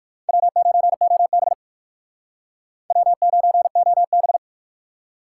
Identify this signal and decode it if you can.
Morse code, heard as a keyed tone, about 56 wpm, 700 Hz. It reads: W9OB W9OB